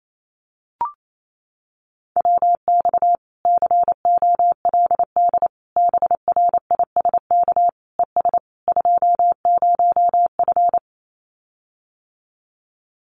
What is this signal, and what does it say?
Morse code, heard as a keyed tone, about 28 wpm, 700 Hz. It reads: WX COLD BRISK ES 20F
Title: Common QSO Phrases Encore 28wpm